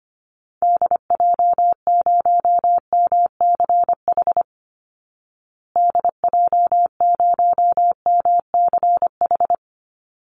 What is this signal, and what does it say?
Morse code, heard as a keyed tone, about 25 wpm, 700 Hz. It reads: DJ0MCH DJ0MCH
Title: Call Signs Repeated 2x - 25wpm